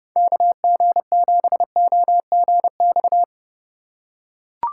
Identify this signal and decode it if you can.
Morse code, heard as a keyed tone, about 29 wpm, 700 Hz. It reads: KG7OGX